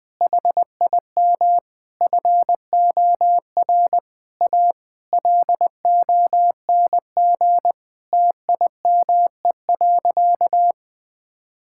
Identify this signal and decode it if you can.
Morse code, heard as a keyed tone, about 20 wpm, 700 Hz. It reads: HIM FOR A LONG TIME.
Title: Sentences with 4-Word Phrases - 20wpm